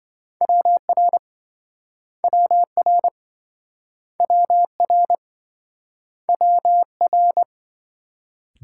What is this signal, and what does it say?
Morse code, heard as a keyed tone, about 27 wpm, 700 Hz. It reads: WR WR WR WR